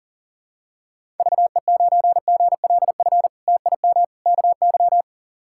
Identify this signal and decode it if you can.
Morse code, heard as a keyed tone, about 40 wpm, 700 Hz. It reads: VE9GLF TIM KY
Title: K1USN SST QSO Practice - 40wpm